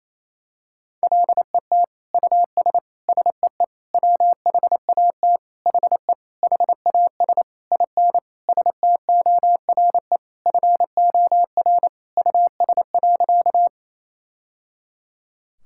Morse code, heard as a keyed tone, about 27 wpm, 700 Hz. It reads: LET US SEE WHAT HE HAS IN STORE FOR US.
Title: Sentences from Top 1000 Words - Review Part 5 - 28wpm